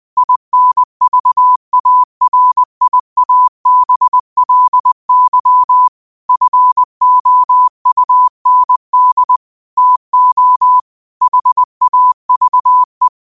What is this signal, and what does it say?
Morse code, heard as a keyed tone, about 20 wpm, 1000 Hz. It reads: INVARIABLY FOUND TO HAVE